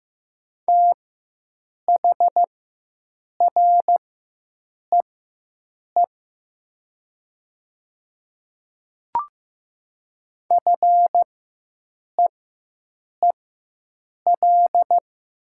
Morse code, heard as a keyed tone, about 15 wpm, spaced as for 7 wpm, 700 Hz. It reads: THREE FEEL
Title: 4x Character Spacing - Top 500 Words - 15wpm